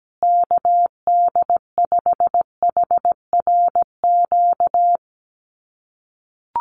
Morse code, heard as a keyed tone, about 17 wpm, 700 Hz. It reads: KD5HRQ
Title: USA Call Signs Repeated 3x 17wpm